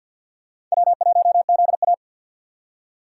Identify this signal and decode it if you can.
Morse code, heard as a keyed tone, about 53 wpm, 700 Hz. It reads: W1ZA
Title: USA Call Signs - 4 Characters - 3x Character Spacing - 50wpm